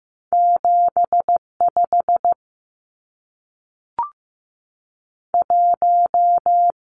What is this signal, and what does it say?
Morse code, heard as a keyed tone, about 15 wpm, 700 Hz. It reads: 75 1
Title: Numbers - 2-Digit 15wpm